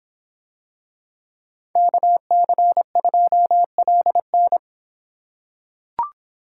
Morse code, heard as a keyed tone, about 26 wpm, 700 Hz. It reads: KC2LN